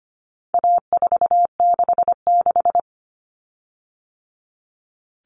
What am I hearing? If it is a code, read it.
Morse code, heard as a keyed tone, about 25 wpm, 700 Hz. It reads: A466
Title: Common Cut Numbers - 4-Digit 25wpm